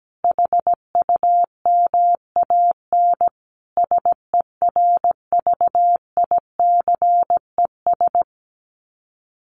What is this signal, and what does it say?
Morse code, heard as a keyed tone, about 17 wpm, 700 Hz. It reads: HUMAN SERVICES
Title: Sets of 5 Words 17wpm